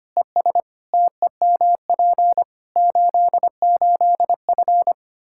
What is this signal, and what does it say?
Morse code, heard as a keyed tone, about 25 wpm, 700 Hz. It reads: ES TEMP 88F
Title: K1USN SST QSO Practice 25wpm